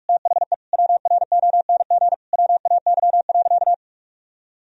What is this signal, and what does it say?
Morse code, heard as a keyed tone, about 46 wpm, 700 Hz. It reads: THE WRONG WAY.
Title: Sentences from Top 1000 Words - Review Part 1 - 45wpm